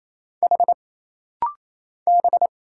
Morse code, heard as a keyed tone, about 27 wpm, 700 Hz. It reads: H B
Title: Single Letter-Number Alt - Rapid-Fire 28wpm